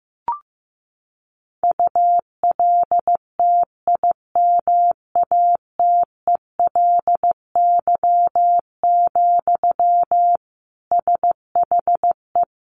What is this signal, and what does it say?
Morse code, heard as a keyed tone, about 14 wpm, 700 Hz. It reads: ULTIMATELY, SHE
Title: Sentences from Top 1200 Words 15wpm